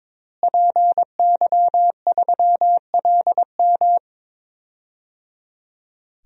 Morse code, heard as a keyed tone, about 21 wpm, 700 Hz. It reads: PY3LM
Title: DXCC-100 Call Signs - 5 Characters - Repeated 3x - 22wpm